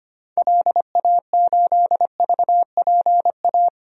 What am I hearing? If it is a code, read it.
Morse code, heard as a keyed tone, about 25 wpm, 700 Hz. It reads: LA8VPA
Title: DXCC-100 Call Signs - 6 Characters - 25wpm